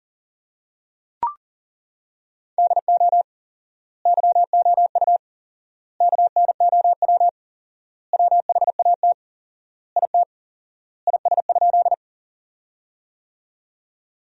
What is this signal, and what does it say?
Morse code, heard as a keyed tone, about 40 wpm, 700 Hz. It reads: DO YOU KNOW WHAT IT IS?
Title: 4x Word-Spacing - Sentences from Top 100 Words - 40wpm